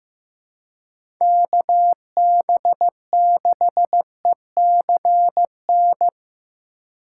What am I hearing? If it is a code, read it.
Morse code, heard as a keyed tone, about 15 wpm, 700 Hz. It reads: KB6ECN
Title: USA Call Signs - 6 Characters - 3x Character Spacing - 15wpm